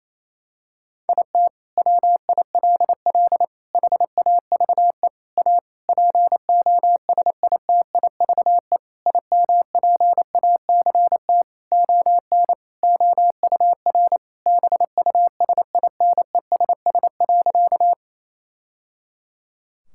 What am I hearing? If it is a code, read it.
Morse code, heard as a keyed tone, about 27 wpm, 700 Hz. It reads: IT WILL HAVE A POSITIVE IMPACT ON OUR BUSINESS.
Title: Sentences from Top 1000 Words - Review Part 6 - 28wpm